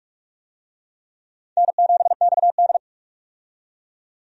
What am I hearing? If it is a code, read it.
Morse code, heard as a keyed tone, about 45 wpm, 700 Hz. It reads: N7XD
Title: USA Call Signs - 4 Characters - 45wpm